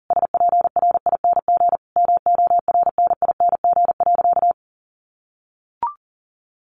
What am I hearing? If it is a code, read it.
Morse code, heard as a keyed tone, about 40 wpm, 700 Hz. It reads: SPRING MORNING.
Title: Sentences from Top 1200 Words 40wpm